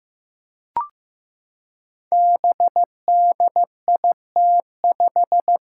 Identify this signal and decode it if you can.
Morse code, heard as a keyed tone, about 15 wpm, 700 Hz. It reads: BDIT5
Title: Five Challenging Letter-Number Combinations 15wpm